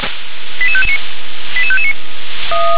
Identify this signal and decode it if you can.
Five-tone selective call (ZVEI), heard as quick runs of five tones, tones of about 70 ms. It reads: 88559 88559
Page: /357/SDR/ZVEI/